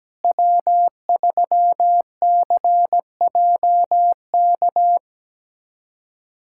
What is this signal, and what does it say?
Morse code, heard as a keyed tone, about 17 wpm, 700 Hz. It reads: W3CJK
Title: USA Callsign DE Callsign - 5 Characters - 17wpm